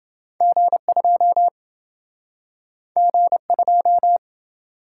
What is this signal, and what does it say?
Morse code, heard as a keyed tone, about 28 wpm, 700 Hz. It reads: G2 G2